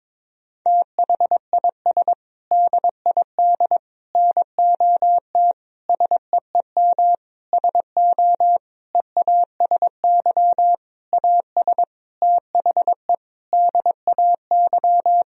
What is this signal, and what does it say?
Morse code, heard as a keyed tone, about 22 wpm, 700 Hz. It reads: THIS DID NOT SEEM SO EASY AS THE DAY
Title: Sentences from Top 1000 Words - Review Part 1 - 22wpm